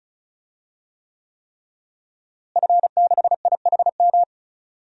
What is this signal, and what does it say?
Morse code, heard as a keyed tone, about 36 wpm, 700 Hz. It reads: F6IHM
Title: DXCC-100 Call Signs - 5 Characters - Repeated 3x - 35wpm